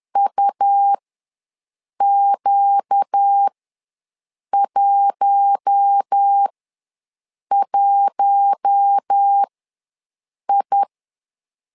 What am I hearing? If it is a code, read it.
Morse code, heard as a keyed tone, about 10 wpm, spaced as for 6 wpm, 800 Hz. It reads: UQ11I
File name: LEZIONI / 4 Q1 ws 10wpm 2538